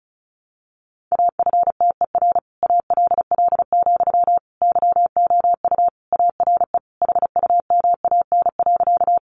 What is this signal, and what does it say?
Morse code, heard as a keyed tone, about 34 wpm, 700 Hz. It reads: AFTER ALL, YOU ARE HUMAN.
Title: Sentences from Top 600 Words 35wpm